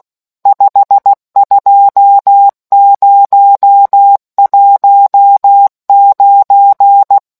Morse code, heard as a keyed tone, about 16 wpm, 800 Hz. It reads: 52019